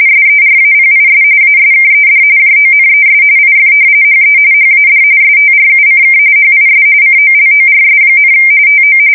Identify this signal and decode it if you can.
RTTY, about 45.45 baud. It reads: SERAPIS ASKED JONES IF HE HAD STRUCK HIS COLORS, TO
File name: fsk170 / usnavy1